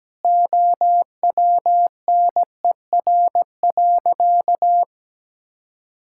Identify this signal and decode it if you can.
Morse code, heard as a keyed tone, about 17 wpm, 700 Hz. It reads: OWNER.
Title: Sentences from Top 1600 Words 17wpm